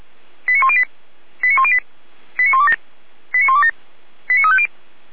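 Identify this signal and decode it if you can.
Five-tone selective call (ZVEI), heard as quick runs of five tones, tones of about 70 ms. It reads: 89108 89108 89127 89127 89350
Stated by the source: ZVEI five-tone selective call received on an SDR.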